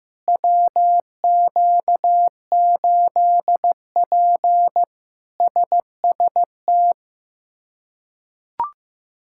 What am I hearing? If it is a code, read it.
Morse code, heard as a keyed tone, about 15 wpm, 700 Hz. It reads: WQ8P SST
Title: K1USN SST QSO Practice - 15wpm